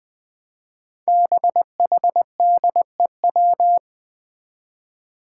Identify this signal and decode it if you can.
Morse code, heard as a keyed tone, about 21 wpm, 700 Hz. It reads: BHDEW
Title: Five Challenging Letter-Number Combinations 20wpm